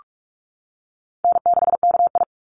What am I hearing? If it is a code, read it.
Morse code, heard as a keyed tone, about 44 wpm, 700 Hz. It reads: N6KI
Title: Call Signs - 45wpm